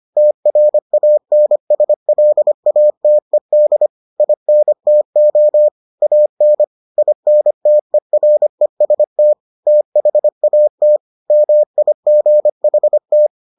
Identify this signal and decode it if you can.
Morse code, heard as a keyed tone, about 25 wpm, 600 Hz. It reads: TRANSLATED INTO AN INTEREST THAT MIGHT